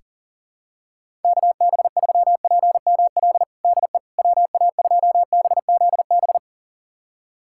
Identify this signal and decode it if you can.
Morse code, heard as a keyed tone, about 40 wpm, 700 Hz. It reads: KB3PML DE WA2BZB
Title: USA Callsign DE Callsign - 6 Characters - 40wpm